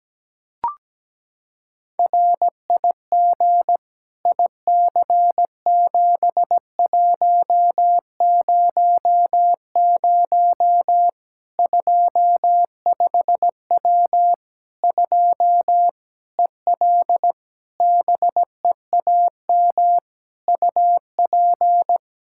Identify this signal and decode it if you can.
Morse code, heard as a keyed tone, about 17 wpm, 700 Hz. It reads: RIG IC7100 25W 2 EL BEAM UP
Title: K1USN SST QSO Practice 17wpm